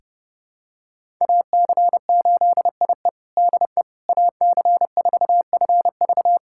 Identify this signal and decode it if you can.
Morse code, heard as a keyed tone, about 30 wpm, 700 Hz. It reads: AC8IE DE AC4FV
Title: USA Callsign DE Callsign - 5 Characters - 30wpm